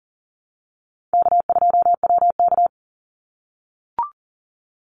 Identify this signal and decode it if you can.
Morse code, heard as a keyed tone, about 42 wpm, 700 Hz. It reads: K2WK
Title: Call Signs - 40wpm